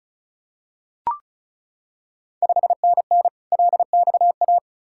Morse code, heard as a keyed tone, about 36 wpm, 700 Hz. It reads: 5NN LXA
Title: Ontario QSO Party Exchange 35wpm